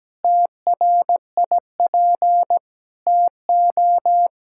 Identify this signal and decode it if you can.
Morse code, heard as a keyed tone, about 17 wpm, 700 Hz. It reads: TRIP TO
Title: Pangram Challenge v2 - 17wpm